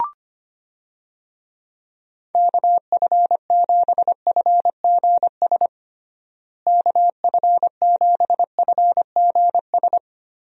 Morse code, heard as a keyed tone, about 25 wpm, 700 Hz. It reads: KF7FGS KF7FGS
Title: USA Call Signs - 6 Characters - Repeated 3x - 25wpm